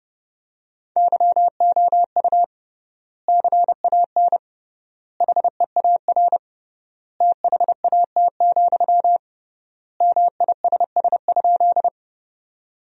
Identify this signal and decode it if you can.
Morse code, heard as a keyed tone, about 30 wpm, 700 Hz. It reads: YOU CAN HEAR THAT, MISS?
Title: 3x Word-Spacing - Sentences from Top 500 Words - 30wpm